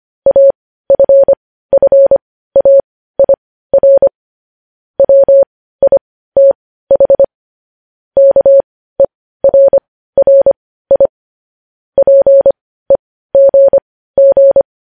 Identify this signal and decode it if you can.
Morse code, heard as a keyed tone, about 25 wpm, spaced as for 15 wpm, 550 Hz. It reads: AFFAIR WITH KERRI PEGG
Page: news-headlines-in-morse-code-at-15-wpm